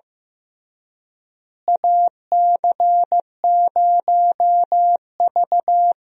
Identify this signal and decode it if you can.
Morse code, heard as a keyed tone, about 15 wpm, 700 Hz. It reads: AC0V
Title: USA Call Signs - 4 Characters - Repeated 3x - 15wpm